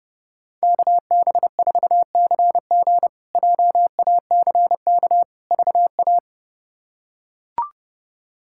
Morse code, heard as a keyed tone, about 32 wpm, 700 Hz. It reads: KB4CG JACK VA
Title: K1USN SST QSO Practice - 30wpm